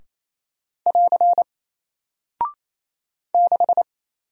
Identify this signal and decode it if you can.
Morse code, heard as a keyed tone, about 28 wpm, 700 Hz. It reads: <AR> 6
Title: Single Character (Letter:Number:Punctuation:Prosign) Alt 28wpm